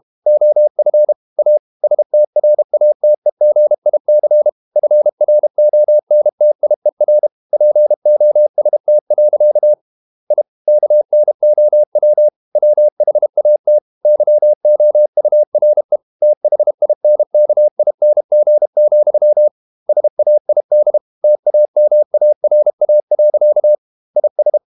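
Morse code, heard as a keyed tone, about 32 wpm, 600 Hz. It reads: OF A STRATEGIC FRONTIER POST. I KNOW WHAT YOURE THINKING, SAID TAMARA. IS
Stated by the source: Morse code practice at 32 wpm from CW med Gnister, reading Ken Follett's novel Never in English.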